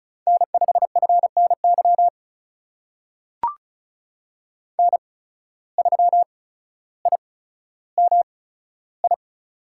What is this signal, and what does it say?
Morse code, heard as a keyed tone, about 36 wpm, spaced as for 15 wpm, 700 Hz. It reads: N5FNY N3IMI
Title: USA Call Signs - 5 Characters - 8x Character Spacing - 35wpm